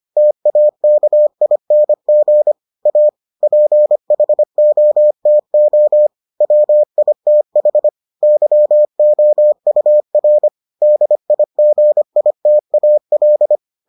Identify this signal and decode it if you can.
Morse code, heard as a keyed tone, about 25 wpm, 600 Hz. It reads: TAKING A PHOTO WITH YOUR DIGITAL